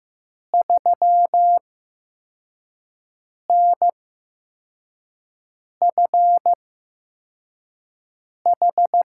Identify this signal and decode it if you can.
Morse code, heard as a keyed tone, about 15 wpm, spaced as for 4 wpm, 700 Hz. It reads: 3NFH
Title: USA Call Signs - 5 Characters - 8x Character Spacing - 15wpm